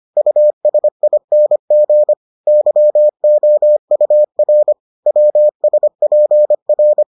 Morse code, heard as a keyed tone, about 25 wpm, 600 Hz. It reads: USING YOUR WSPR